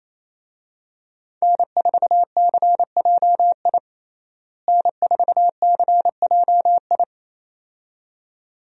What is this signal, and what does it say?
Morse code, heard as a keyed tone, about 28 wpm, 700 Hz. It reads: N4CJI N4CJI
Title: USA Call Signs - 5 Characters - Repeated 2x - 28wpm